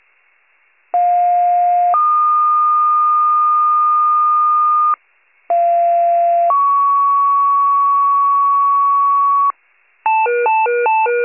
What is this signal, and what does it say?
Two-tone page, heard as a one-second tone, then a three-second tone, about 710 then 1220 Hz, about 690 then 1120 Hz. It is heard from 0.9 to 9.5 s.